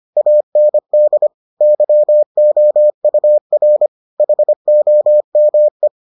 Morse code, heard as a keyed tone, about 25 wpm, 600 Hz. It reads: AND YOUR HOME